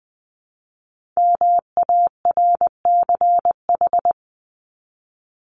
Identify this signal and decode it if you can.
Morse code, heard as a keyed tone, about 20 wpm, 700 Hz. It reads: MARCH
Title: Top 1501-1750 Words 20wpm